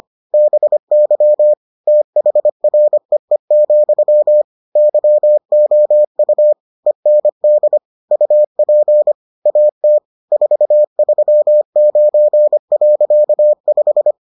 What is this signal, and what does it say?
Morse code, heard as a keyed tone, about 25 wpm, 600 Hz. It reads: BY THREE, YOU END UP AT 439.5